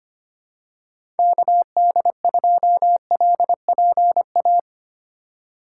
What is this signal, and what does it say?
Morse code, heard as a keyed tone, about 25 wpm, 700 Hz. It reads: KD2LPA
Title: USA Call Signs - 6 Characters - 2x Character Spacing - 25wpm